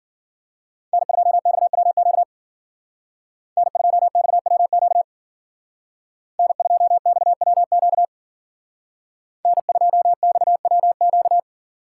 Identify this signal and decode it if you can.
Morse code, heard as a keyed tone, about 49 wpm, 700 Hz. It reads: N2XWQ N2XWQ N2XWQ N2XWQ
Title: Speed Racing - USA 1x3 Call Signs 40wpm